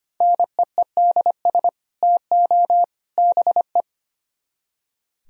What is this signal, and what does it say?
Morse code, heard as a keyed tone, about 30 wpm, 700 Hz. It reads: NEEDS TO BE
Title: Speed-Racing - Sets of 3 Words 25wpm